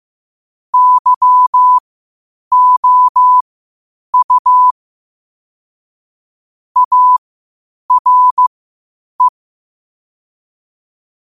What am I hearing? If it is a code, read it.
Morse code, heard as a keyed tone, about 15 wpm, spaced as for 8 wpm, 1000 Hz. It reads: YOU ARE
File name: qotdfiles / 8wpm